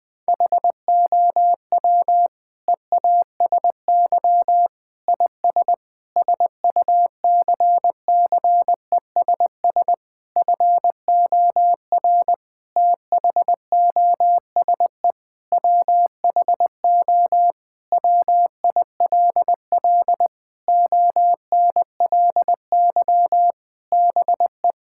Morse code, heard as a keyed tone, about 20 wpm, 700 Hz. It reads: HOW EASY IS SUCCESS FOR THOSE WHO WILL ONLY BE
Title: Sentences from Top 1000 Words - Review Part 6 - 20wpm